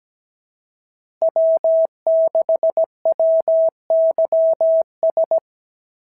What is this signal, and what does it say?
Morse code, heard as a keyed tone, about 16 wpm, 650 Hz. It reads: W6WYS
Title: SOTA QSO Practice 17wpm